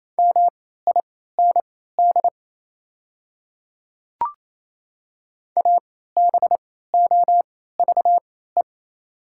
Morse code, heard as a keyed tone, about 28 wpm, spaced as for 16 wpm, 700 Hz. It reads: MIND ABOVE
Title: 3x Character Spacing - Top 500 Words - 28wpm